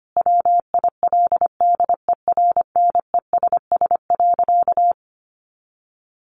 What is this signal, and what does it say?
Morse code, heard as a keyed tone, about 25 wpm, 700 Hz. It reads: WILDERNESS.
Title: A Sound Of Thunder 25wpm